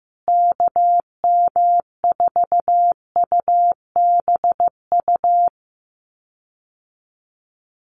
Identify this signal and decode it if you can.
Morse code, heard as a keyed tone, about 14 wpm, 700 Hz. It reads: KM4UBU
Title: USA Call Signs Repeated 3x 15wpm